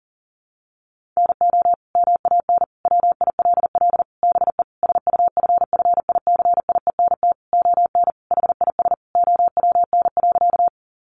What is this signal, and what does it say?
Morse code, heard as a keyed tone, about 40 wpm, 700 Hz. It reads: NO MAN WILL BE SUFFICIENT ON HIS OWN.
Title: Sentences from Top 1400 Words 40wpm